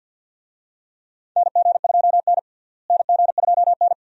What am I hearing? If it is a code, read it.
Morse code, heard as a keyed tone, about 50 wpm, 700 Hz. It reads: NG2N NG2N
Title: USA Call Signs - 4 Characters - Repeated 2x - 50wpm